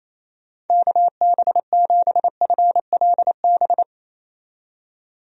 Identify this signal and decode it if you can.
Morse code, heard as a keyed tone, about 28 wpm, 700 Hz. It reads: KB7FLB
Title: USA Call Signs - 6 Characters - 28wpm